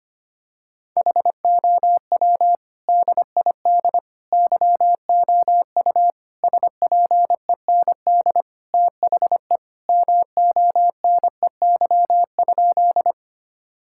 Morse code, heard as a keyed tone, about 25 wpm, 700 Hz. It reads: HOW DID YOU SPEND THE MONEY?
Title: Sentences from Top 1000 Words - Review Part 5 - 25wpm